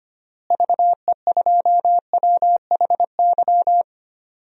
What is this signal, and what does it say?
Morse code, heard as a keyed tone, about 25 wpm, 700 Hz. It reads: VE2WHY
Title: DXCC-100 Call Signs - 6 Characters - Repeated 3x - 25wpm